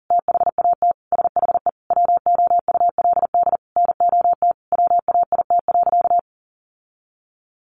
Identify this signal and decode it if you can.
Morse code, heard as a keyed tone, about 40 wpm, 700 Hz. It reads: THAT SHE WOULD NOT WAIT.
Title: Sentences from Top 1400 Words 40wpm